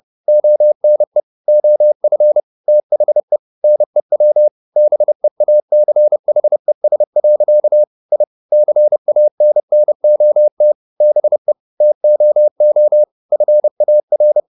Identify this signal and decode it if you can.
Morse code, heard as a keyed tone, about 30 wpm, 600 Hz. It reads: ONE OF THE NEW BEACHES. I CANNOT BE TOO FAR